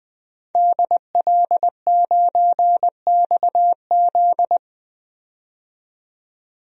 Morse code, heard as a keyed tone, about 21 wpm, 700 Hz. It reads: DL9XZ
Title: DXCC-100 Call Signs - 5 Characters - 20wpm